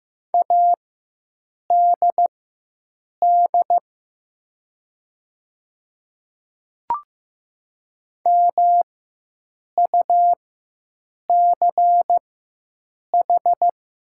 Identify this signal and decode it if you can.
Morse code, heard as a keyed tone, about 15 wpm, spaced as for 7 wpm, 700 Hz. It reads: ADD MUCH
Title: 4x Character Spacing - Top 200 Words - 15wpm